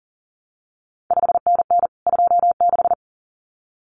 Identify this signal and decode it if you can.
Morse code, heard as a keyed tone, about 40 wpm, 700 Hz. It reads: 5NN 26
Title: CQ WW DX Contest Exchange Encore - 40wpm